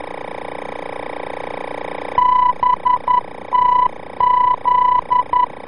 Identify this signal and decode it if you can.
Morse code, heard as a keyed tone, about 11 wpm, 990 Hz. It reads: BTZ
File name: BTZ-VOR-F